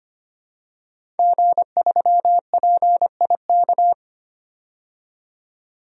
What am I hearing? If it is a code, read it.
Morse code, heard as a keyed tone, about 25 wpm, 700 Hz. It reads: G3PIK
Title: DXCC-100 Call Signs - 5 Characters - 25wpm